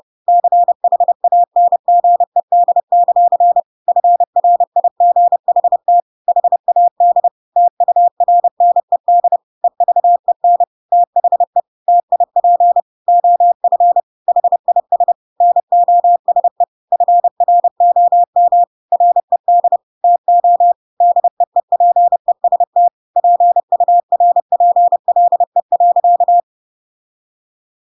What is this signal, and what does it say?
Morse code, heard as a keyed tone, about 30 wpm, 700 Hz. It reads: CHANGED; FRIGHT HAD TURNED EVEN THE TIP OF HIS NOSE FROM RED TO DEEPEST PURPLE.
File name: Pinocchio / Pinocchio ch1 30 30 ik1hge